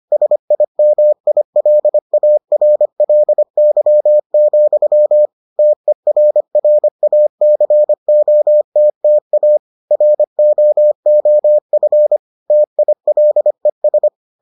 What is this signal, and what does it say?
Morse code, heard as a keyed tone, about 25 wpm, 600 Hz. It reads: SIMILARLY, TERRACOTTA ROOF TILES